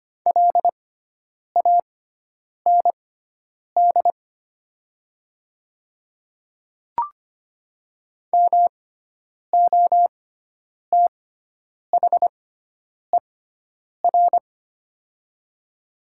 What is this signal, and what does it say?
Morse code, heard as a keyed tone, about 25 wpm, spaced as for 9 wpm, 700 Hz. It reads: LAND MOTHER
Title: 6x Character Spacing - Top 200 Words - 25wpm